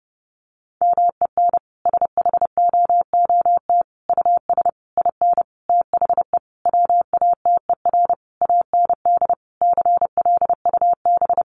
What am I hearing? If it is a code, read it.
Morse code, heard as a keyed tone, about 30 wpm, 700 Hz. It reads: MEN SHOOT US IN THE WATER AND CLUB
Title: Sentences from Top 1200 Words 30wpm